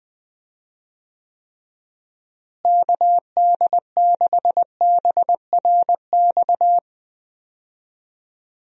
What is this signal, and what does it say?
Morse code, heard as a keyed tone, about 20 wpm, 700 Hz. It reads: KD6BRX
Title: USA Call Signs - 6 Characters - Repeated 3x - 20wpm